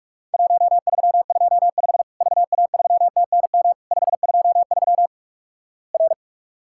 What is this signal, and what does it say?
Morse code, heard as a keyed tone, about 46 wpm, 700 Hz. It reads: 1325 VA3TNM 523 R
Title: Worked All Europe DX Contest - QTC Exchange - 45wpm